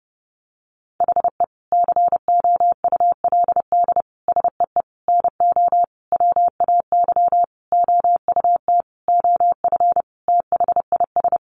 Morse code, heard as a keyed tone, about 30 wpm, 700 Hz. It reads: HE COULD SEE NO WAY OUT OF THIS
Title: Sentences from Top 1200 Words 30wpm